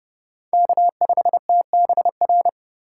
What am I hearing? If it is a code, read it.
Morse code, heard as a keyed tone, about 32 wpm, 700 Hz. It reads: K5TBR
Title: USA Call Signs - 5 Characters - 4x Character Spacing - 30wpm